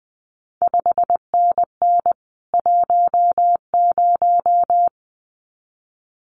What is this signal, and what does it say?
Morse code, heard as a keyed tone, about 19 wpm, 700 Hz. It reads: 5NN 10
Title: CQ WW DX Contest Exchange Encore - 20wpm